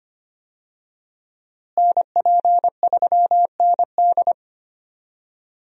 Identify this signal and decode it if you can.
Morse code, heard as a keyed tone, about 25 wpm, 700 Hz. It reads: NP3ND
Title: USA Call Signs - 5 Characters - Repeated 3x - 25wpm